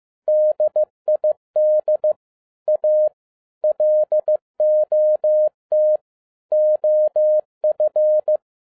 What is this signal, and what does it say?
Morse code, heard as a keyed tone, about 15 wpm, 610 Hz. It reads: DID A LOT OF
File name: FIELD DAY CONTEST QRV FROM SAKAE VILLAGE 15wpm